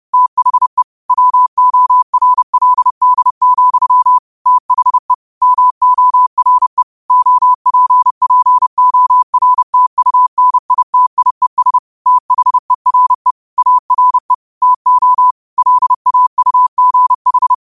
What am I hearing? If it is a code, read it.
Morse code, heard as a keyed tone, about 30 wpm, 1000 Hz. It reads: THE WORLD, THE MORE OPPORTUNITIES THERE ARE TO LAUGH
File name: qotdfiles / 30wpm